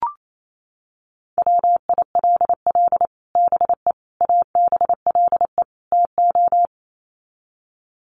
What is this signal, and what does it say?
Morse code, heard as a keyed tone, about 28 wpm, 700 Hz. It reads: WILL BE ABLE TO
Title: Sets of 4 Words 28wpm